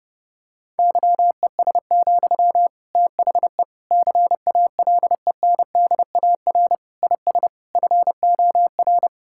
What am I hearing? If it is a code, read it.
Morse code, heard as a keyed tone, about 30 wpm, 700 Hz. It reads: YES, THE CALENDAR IS FOR
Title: Sentences from Top 1600 Words 30wpm